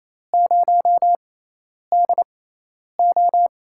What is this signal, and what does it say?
Morse code, heard as a keyed tone, about 28 wpm, spaced as for 10 wpm, 700 Hz. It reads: 0DO
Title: USA Call Signs - 4 Characters - 6x Character Spacing - 28wpm